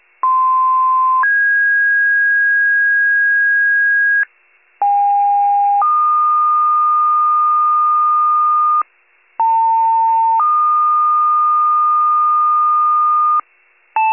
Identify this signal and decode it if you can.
Two-tone page, heard as a one-second tone, then a three-second tone, about 1040 then 1650 Hz, about 810 then 1220 Hz, about 900 then 1220 Hz. It is heard from 0.2 to 13.4 s.
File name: pager / 78 RESCUE / wcalert